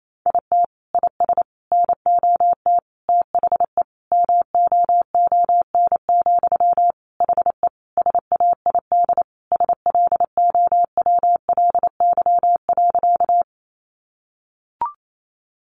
Morse code, heard as a keyed tone, about 28 wpm, 700 Hz. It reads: IT IS NOT THE MOON, HE SAID SLOWLY.
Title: Sentences from Top 1100 Words 28wpm